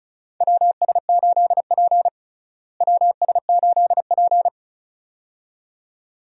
Morse code, heard as a keyed tone, about 34 wpm, 700 Hz. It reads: WS8P WS8P
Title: USA Call Signs - 4 Characters - Repeated 2x - 35wpm